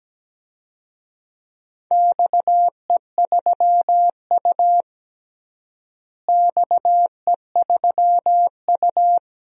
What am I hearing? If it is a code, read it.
Morse code, heard as a keyed tone, about 17 wpm, 700 Hz. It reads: XE3U XE3U
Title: DXCC-100 Call Signs - 4 Characters - Repeated 2x - 17wpm